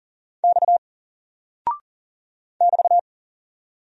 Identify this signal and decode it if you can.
Morse code, heard as a keyed tone, about 39 wpm, 700 Hz. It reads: X <BT>
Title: Single Character (Letter:Number:Punctuation:Prosign) 40wpm